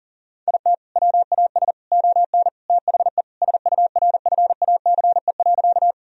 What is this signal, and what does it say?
Morse code, heard as a keyed tone, about 40 wpm, 700 Hz. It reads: IT WAS ON THE SURFACE.
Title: Sentences from Top 1000 Words - Review Part 3 - 40wpm